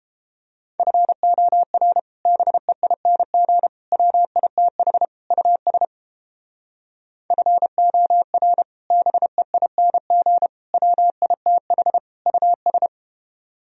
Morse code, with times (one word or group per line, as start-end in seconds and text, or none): 0.8-2.0 FOR
2.3-3.7 BEING
3.9-5.1 WITH
5.3-5.9 US
7.3-8.6 FOR
8.9-10.5 BEING
10.7-12.0 WITH
12.3-12.9 US